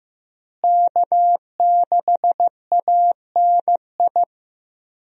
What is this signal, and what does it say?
Morse code, heard as a keyed tone, about 15 wpm, 700 Hz. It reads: K6ANI
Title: USA Call Signs - 5 Characters - Repeated 3x - 15wpm